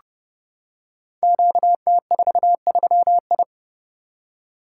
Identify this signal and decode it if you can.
Morse code, heard as a keyed tone, about 30 wpm, 700 Hz. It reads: QT43I
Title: Code Groups - 5 Letter:Numbers - 30wpm